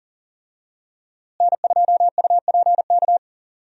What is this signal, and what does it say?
Morse code, heard as a keyed tone, about 37 wpm, 700 Hz. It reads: N2UPK
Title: USA Call Signs - 5 Characters - Repeated 3x - 40wpm